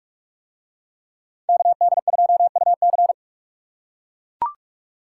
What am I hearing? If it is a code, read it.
Morse code, heard as a keyed tone, about 43 wpm, 700 Hz. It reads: KD2UC